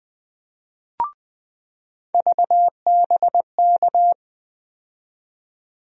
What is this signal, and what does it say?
Morse code, heard as a keyed tone, about 20 wpm, 700 Hz. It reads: VBK
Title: USA Call Sign Postfixes - 3 Letters 20wpm